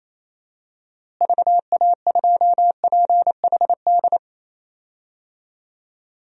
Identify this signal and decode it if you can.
Morse code, heard as a keyed tone, about 28 wpm, 700 Hz. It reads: VA2PHD
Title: DXCC-100 Call Signs - 6 Characters - Repeated 2x - 28wpm